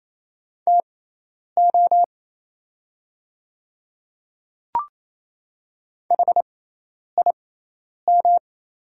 Morse code, heard as a keyed tone, about 28 wpm, spaced as for 10 wpm, 690 Hz. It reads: TO HIM
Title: 6x Character Spacing - Top 100 Words - 28wpm